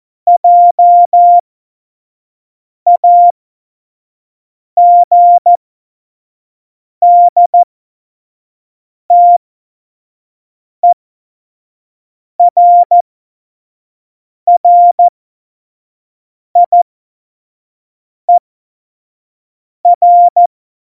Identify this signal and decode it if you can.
Morse code, heard as a keyed tone, about 14 wpm, spaced as for 5 wpm, 700 Hz. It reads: JAGDTERRIER
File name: Morsekurs / Witze / Witz 27 Juni 17, Tempo 25 F70